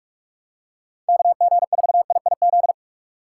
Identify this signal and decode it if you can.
Morse code, heard as a keyed tone, about 46 wpm, 700 Hz. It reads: KG4IIZ
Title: USA Call Signs - 6 Characters - 6x Character Spacing - 45wpm